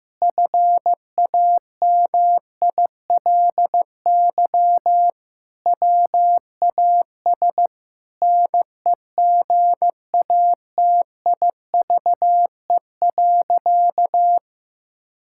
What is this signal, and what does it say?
Morse code, heard as a keyed tone, about 16 wpm, 700 Hz. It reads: FAMILY WAS NEGATIVE.
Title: Sentences from Top 1000 Words - Review Part 6 - 15wpm